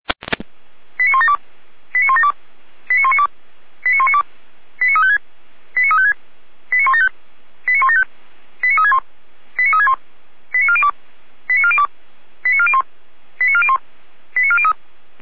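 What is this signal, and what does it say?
Five-tone selective call (ZVEI), heard as quick runs of five tones, tones of about 70 ms. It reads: 89172 89172 89192 89192 89356 89356 89175 89175 89361 89361 89402 89402 89401 89401 89403
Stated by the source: ZVEI five-tone selective call received with an SDR.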